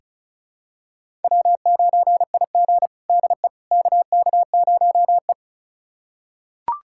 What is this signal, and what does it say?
Morse code, heard as a keyed tone, about 34 wpm, 700 Hz. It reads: W9IG DE KK0E
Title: USA Callsign DE Callsign - 4 Characters - 35wpm